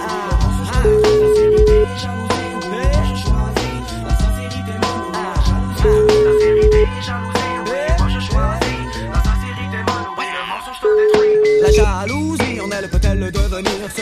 Ringback tone (Europe, 425 Hz), with noise.